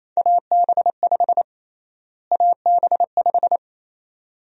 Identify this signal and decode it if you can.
Morse code, heard as a keyed tone, about 28 wpm, 700 Hz. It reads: AB5 AB5
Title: USA Call Sign Prefixes - Repeated 2x - 28wpm